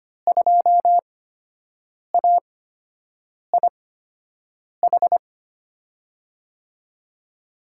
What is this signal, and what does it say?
Morse code, heard as a keyed tone, about 25 wpm, spaced as for 7 wpm, 700 Hz. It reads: 2AIH
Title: USA Call Signs - 8x Character Spacing - 25wpm